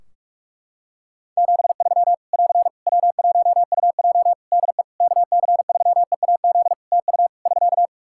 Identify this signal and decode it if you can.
Morse code, heard as a keyed tone, about 46 wpm, 700 Hz. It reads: 73 <AR> W1UJ DE KC3EAZ TU <SK>
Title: Practice QSO - Exchange 4 - Ending - 45wpm